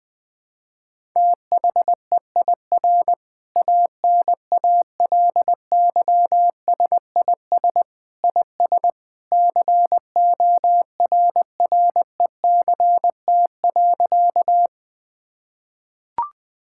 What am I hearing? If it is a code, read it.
Morse code, heard as a keyed tone, about 20 wpm, 700 Hz. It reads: THEIR ANALYSIS IS CORRECT.
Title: Sentences from Top 1000 Words - Review Part 5 - 20wpm